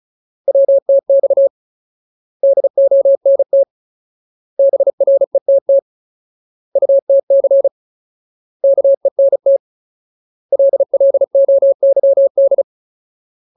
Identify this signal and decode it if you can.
Morse code, heard as a keyed tone, about 34 wpm, 550 Hz. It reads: WTX DONT BRETT UTC KENT LLOYD
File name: Ultimate Word List / 35WPM Ultimate